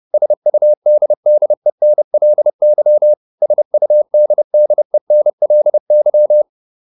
Morse code, heard as a keyed tone, about 30 wpm, 600 Hz. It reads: SUDDENLY SUDDENLY